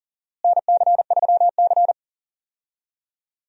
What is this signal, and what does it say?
Morse code, heard as a keyed tone, about 40 wpm, 700 Hz. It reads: NC3C
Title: USA Call Signs - 4 Characters - Repeated 3x - 40wpm